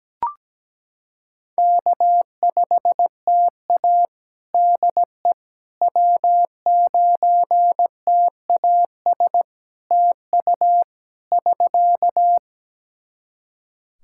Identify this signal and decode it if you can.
Morse code, heard as a keyed tone, about 17 wpm, 700 Hz. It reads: K5TA DE W9TAS TU <SK>
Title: Practice QSO - Exchange 4 - Ending - 17wpm